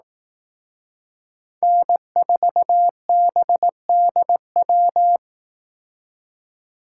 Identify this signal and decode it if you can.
Morse code, heard as a keyed tone, about 18 wpm, 700 Hz. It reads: N4BDW
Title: Speed Racing - USA Call Signs 15wpm